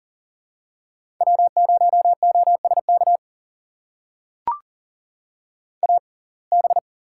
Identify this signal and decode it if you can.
Morse code, heard as a keyed tone, about 40 wpm, spaced as for 29 wpm, 690 Hz. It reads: W0OSK AB
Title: USA Call Signs - 5 Characters - 6x Character Spacing - 40wpm